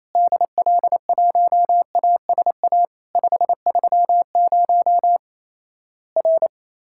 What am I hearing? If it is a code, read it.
Morse code, heard as a keyed tone, about 28 wpm, 700 Hz. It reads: DL1ASA 530 R
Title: Worked All Europe DX Contest - QTC Exchange - 28wpm